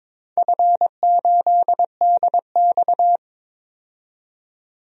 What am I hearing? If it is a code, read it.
Morse code, heard as a keyed tone, about 21 wpm, 700 Hz. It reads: F8DX
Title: DXCC-100 Call Signs - 4 Characters - 22wpm